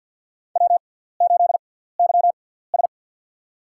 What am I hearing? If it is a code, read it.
Morse code, heard as a keyed tone, about 50 wpm, spaced as for 17 wpm, 700 Hz. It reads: W8YS
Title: USA Call Signs - 4 Characters - 6x Character Spacing - 50wpm